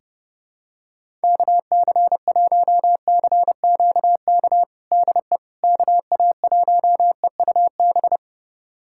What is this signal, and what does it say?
Morse code, heard as a keyed tone, about 30 wpm, 700 Hz. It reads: KC1CQK DE KA1EUB
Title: USA Callsign DE Callsign 30wpm